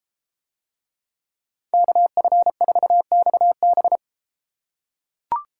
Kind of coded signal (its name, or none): Morse code